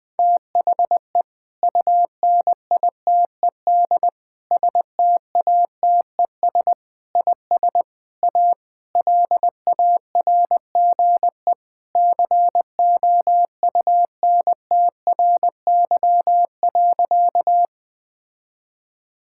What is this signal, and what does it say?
Morse code, heard as a keyed tone, about 20 wpm, 700 Hz. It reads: THE UNITED STATES IS A LARGE COUNTRY.
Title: Sentences with 4-Word Phrases - 20wpm